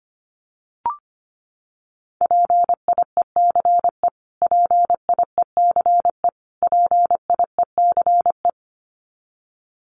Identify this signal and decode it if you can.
Morse code, heard as a keyed tone, about 25 wpm, 700 Hz. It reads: PIECE PIECE PIECE
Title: Top 500 Words Repeated 25wpm